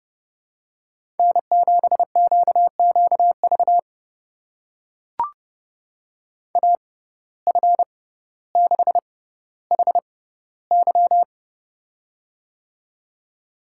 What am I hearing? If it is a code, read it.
Morse code, heard as a keyed tone, about 32 wpm, spaced as for 16 wpm, 700 Hz. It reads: N7QQV AF6HY
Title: USA Call Signs - 5 Characters - 6x Character Spacing - 30wpm